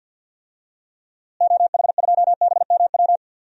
Morse code, heard as a keyed tone, about 50 wpm, 700 Hz. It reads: OH2BMW
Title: DXCC-100 Call Signs - 6 Characters - Repeated 2x - 50wpm